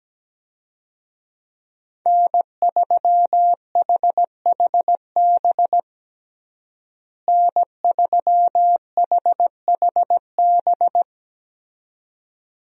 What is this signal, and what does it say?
Morse code, heard as a keyed tone, about 17 wpm, 700 Hz. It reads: N3HHB N3HHB
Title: USA Call Signs - 5 Characters - Repeated 2x - 17wpm